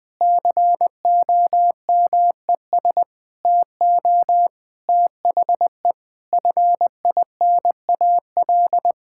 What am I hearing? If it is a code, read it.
Morse code, heard as a keyed tone, about 20 wpm, 700 Hz. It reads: COMES TO THE FINAL
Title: Sentences with 5-Word Phrases - 20wpm